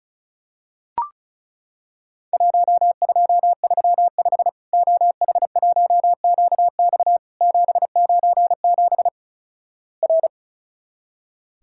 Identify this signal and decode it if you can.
Morse code, heard as a keyed tone, about 36 wpm, 700 Hz. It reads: 1235 OH1QX 797 R
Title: Worked All Europe DX Contest - QTC Exchange - 35wpm